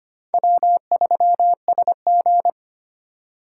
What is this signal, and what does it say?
Morse code, heard as a keyed tone, about 33 wpm, 700 Hz. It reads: W3SG